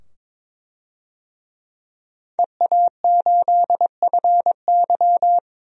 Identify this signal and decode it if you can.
Morse code, heard as a keyed tone, about 21 wpm, 700 Hz. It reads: EA8FY
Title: DXCC-100 Call Signs - 5 Characters - 22wpm